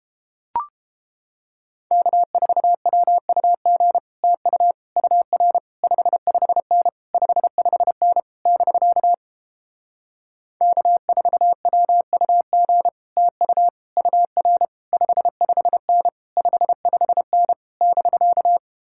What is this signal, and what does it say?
Morse code, heard as a keyed tone, about 31 wpm, 700 Hz. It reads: K4WUG TU UR 55N 55N <BK> K4WUG TU UR 55N 55N <BK>